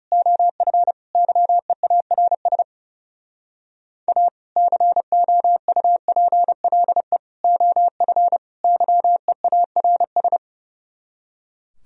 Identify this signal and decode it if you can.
Morse code, heard as a keyed tone, about 37 wpm, 700 Hz. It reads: OF YEARS A COUPLE OF YEARS
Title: Speed-Racing - Sets of 4 Words 30wpm